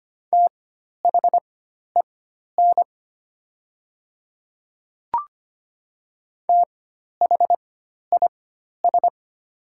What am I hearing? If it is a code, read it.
Morse code, heard as a keyed tone, about 25 wpm, spaced as for 12 wpm, 700 Hz. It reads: THEN THIS
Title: 4x Character Spacing - Top 200 Words - 25wpm